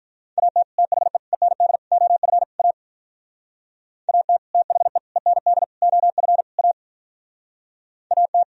Morse code, heard as a keyed tone, about 46 wpm, 700 Hz. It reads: AT THE END OF A AT THE END OF A AT